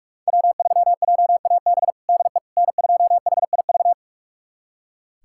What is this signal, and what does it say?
Morse code, heard as a keyed tone, about 45 wpm, 700 Hz. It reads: W3JAB DE N2HIV
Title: USA Callsign DE Callsign - 5 Characters - 45wpm